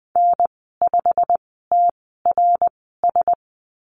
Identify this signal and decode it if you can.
Morse code, heard as a keyed tone, about 20 wpm, spaced as for 15 wpm, 700 Hz. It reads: N5TRS
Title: 088 Lesson 15g - Number 5 - Call Signs 20wpm